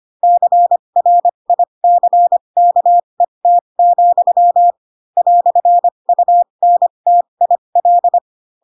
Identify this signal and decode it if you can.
Morse code, heard as a keyed tone, about 25 wpm, spaced as for 22 wpm, 700 Hz. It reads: CRICKET, "UNTIL
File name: Pinocchio / Pinocchio ch4 18 25 ik1hge